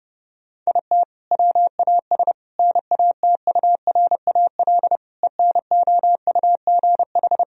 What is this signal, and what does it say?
Morse code, heard as a keyed tone, about 30 wpm, 700 Hz. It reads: IT WAS NATURAL ENOUGH
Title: Sentences from Top 1000 Words - Review Part 4 - 30wpm